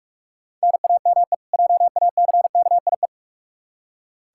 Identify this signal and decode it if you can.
Morse code, heard as a keyed tone, about 43 wpm, 700 Hz. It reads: NAME JACKIE
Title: Practice QSO - Exchange 1 - Beginning - 45wpm